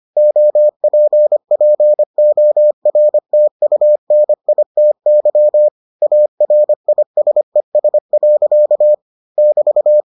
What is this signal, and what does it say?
Morse code, heard as a keyed tone, about 25 wpm, 600 Hz. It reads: OPPORTUNITY ARISES. <BT>